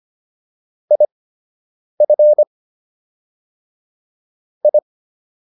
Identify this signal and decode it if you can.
Morse code, heard as a keyed tone, about 25 wpm, spaced as for 8 wpm, 600 Hz. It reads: IF I